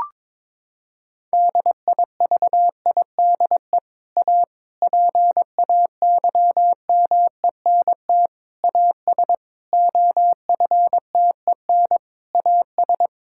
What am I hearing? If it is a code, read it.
Morse code, heard as a keyed tone, about 22 wpm, 700 Hz. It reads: DIVIDE A PAYMENT AS OFTEN AS
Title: Sentences from Top 1600 Words 22wpm